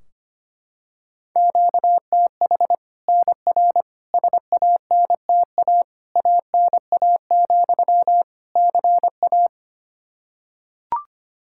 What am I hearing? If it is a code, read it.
Morse code, heard as a keyed tone, about 25 wpm, 700 Hz. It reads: QTH NR SANTA ANA, CA
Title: Practice QSO - Exchange 1 - Beginning - 25wpm